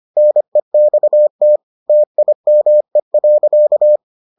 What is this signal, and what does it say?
Morse code, heard as a keyed tone, about 25 wpm, 600 Hz. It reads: NEXT TIME.